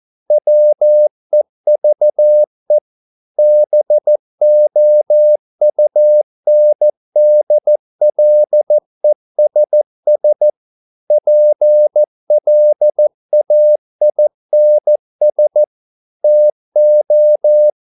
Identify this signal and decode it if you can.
Morse code, heard as a keyed tone, about 14 wpm, 600 Hz. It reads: WEVE BOUNDLESS PLAINS TO